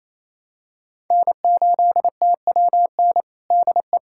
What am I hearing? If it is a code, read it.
Morse code, heard as a keyed tone, about 28 wpm, 700 Hz. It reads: N8TWN DE
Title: USA Callsign DE Callsign 28wpm